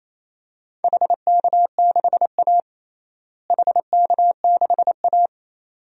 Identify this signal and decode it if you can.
Morse code, heard as a keyed tone, about 28 wpm, 700 Hz. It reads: HK6A HK6A
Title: DXCC-100 Call Signs - 4 Characters - Repeated 2x - 28wpm